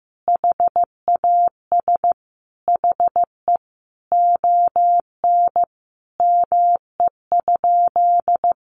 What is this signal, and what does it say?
Morse code, heard as a keyed tone, about 15 wpm, 700 Hz. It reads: HAS HE ON ME?
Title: Sentences from Top 1500 Words 15wpm